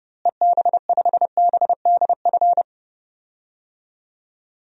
Morse code, heard as a keyed tone, about 32 wpm, 700 Hz. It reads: EB5BDF
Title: DXCC-100 Call Signs - 6 Characters - Repeated 3x - 30wpm